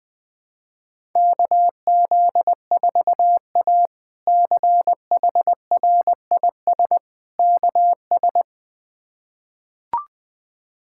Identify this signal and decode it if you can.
Morse code, heard as a keyed tone, about 20 wpm, 700 Hz. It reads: KZ4A CHRIS KS
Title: K1USN SST QSO Practice - 20wpm